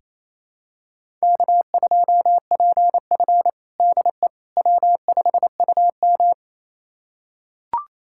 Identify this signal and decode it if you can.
Morse code, heard as a keyed tone, about 28 wpm, 700 Hz. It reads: K2PF DE W5UM
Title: USA Callsign DE Callsign - 4 Characters - 28wpm